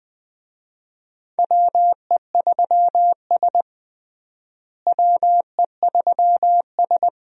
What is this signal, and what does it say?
Morse code, heard as a keyed tone, about 20 wpm, 700 Hz. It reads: WE3S WE3S
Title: USA Call Signs - 4 Characters - Repeated 2x - 20wpm